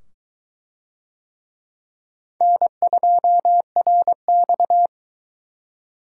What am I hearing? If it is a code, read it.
Morse code, heard as a keyed tone, about 22 wpm, 700 Hz. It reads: N2RX